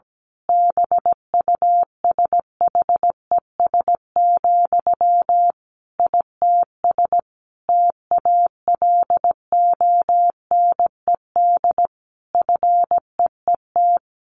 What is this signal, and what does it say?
Morse code, heard as a keyed tone, about 17 wpm, 700 Hz. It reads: BUSHES, ITS TALONED FEET
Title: A Sound Of Thunder 17wpm